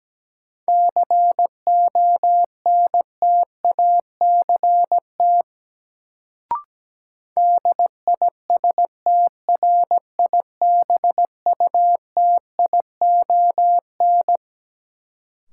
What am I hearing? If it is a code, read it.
Morse code, heard as a keyed tone, about 17 wpm, 700 Hz. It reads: CONTACT DISTRIBUTION
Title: Sentences from Top 1000 Words - Review Part 7 - New Word Drill - 17wpm